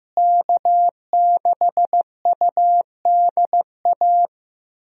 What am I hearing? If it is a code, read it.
Morse code, heard as a keyed tone, about 15 wpm, 700 Hz. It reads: K6UDA
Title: USA Call Signs - 5 Characters - Repeated 2x - 15wpm